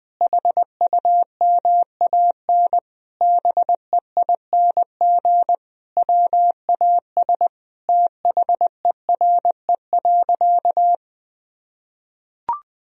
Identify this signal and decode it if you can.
Morse code, heard as a keyed tone, about 20 wpm, 700 Hz. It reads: HUMAN BEING WAS THERE.
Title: Sentences from Top 1000 Words - Review Part 4 - 20wpm